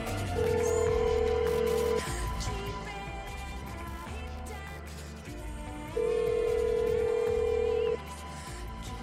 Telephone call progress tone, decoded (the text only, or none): ringback tone